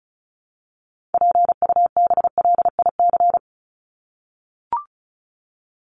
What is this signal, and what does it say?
Morse code, heard as a keyed tone, about 34 wpm, 700 Hz. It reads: PUBLIC
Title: Top 501-750 Words 35wpm